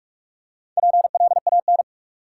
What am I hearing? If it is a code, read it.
Morse code, heard as a keyed tone, about 46 wpm, 700 Hz. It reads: PLAN
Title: Four Letter Words - 45wpm